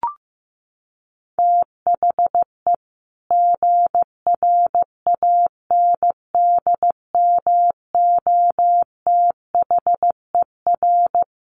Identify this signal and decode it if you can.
Morse code, heard as a keyed tone, about 15 wpm, 700 Hz. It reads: THE GRANDMOTHER
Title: Sentences from Top 1200 Words 15wpm